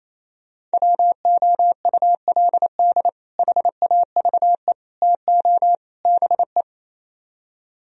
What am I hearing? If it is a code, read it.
Morse code, heard as a keyed tone, about 37 wpm, 700 Hz. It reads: WOULD HAVE TO BE